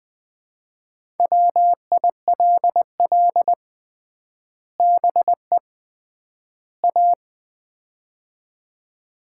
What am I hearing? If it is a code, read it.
Morse code, heard as a keyed tone, about 21 wpm, 700 Hz. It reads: WILL BE A
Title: 3x Word-Spacing - Sets of 3 Words - 20wpm